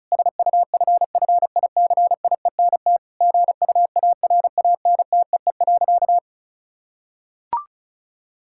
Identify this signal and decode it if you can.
Morse code, heard as a keyed tone, about 36 wpm, 700 Hz. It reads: SUFFICIENT GUARANTEE.
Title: Sentences from Top 1600 Words 35wpm